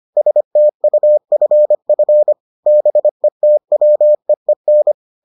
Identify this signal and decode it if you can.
Morse code, heard as a keyed tone, about 25 wpm, 600 Hz. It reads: STUFF BETWEEN